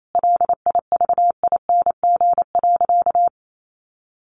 Morse code, heard as a keyed tone, about 28 wpm, 700 Hz. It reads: LIVING.
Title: Sentences from Top 1400 Words 28wpm